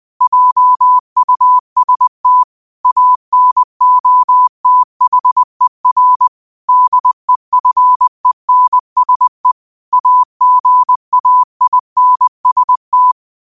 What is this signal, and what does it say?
Morse code, heard as a keyed tone, about 20 wpm, 1000 Hz. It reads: JUST ANOTHER DEFENSE AGAINST
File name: qotdfiles / 20wpm